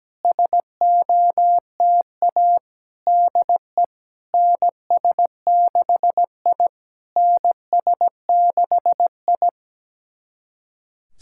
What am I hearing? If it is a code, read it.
Morse code, heard as a keyed tone, about 17 wpm, 700 Hz. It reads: SOTA DE NS6I NS6I
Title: SOTA QSO Practice 17wpm